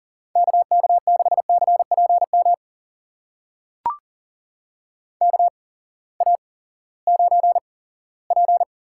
Morse code, heard as a keyed tone, about 37 wpm, spaced as for 20 wpm, 700 Hz. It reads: KK6CPM KA9P
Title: USA Call Signs - 6 Characters - 8x Character Spacing - 40wpm